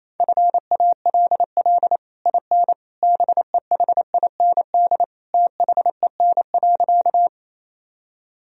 Morse code, heard as a keyed tone, about 28 wpm, 700 Hz. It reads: FALL IN BEHIND THEN.
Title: Sentences from Top 1000 Words - Review Part 2 - 28wpm